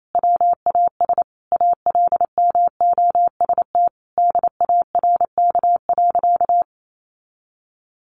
Morse code, heard as a keyed tone, about 28 wpm, 700 Hz. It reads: WAS ALMOST DARK.
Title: Sentences from Top 700 Words 28wpm